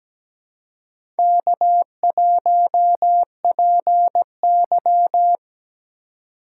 Morse code, heard as a keyed tone, about 17 wpm, 700 Hz. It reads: K1PY
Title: USA Call Signs - 2x Character Spacing - 17wpm